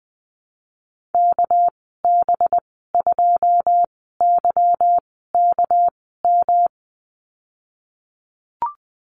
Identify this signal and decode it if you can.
Morse code, heard as a keyed tone, about 20 wpm, spaced as for 14 wpm, 700 Hz. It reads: KB2YKM
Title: 245 Lesson 39g - Letter z - Call Signs 20wpm